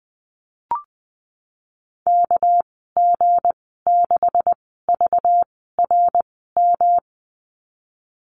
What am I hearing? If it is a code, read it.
Morse code, heard as a keyed tone, about 20 wpm, spaced as for 15 wpm, 700 Hz. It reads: KG6VRM
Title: 231 Lesson 37g - Letter x - Call Signs 20wpm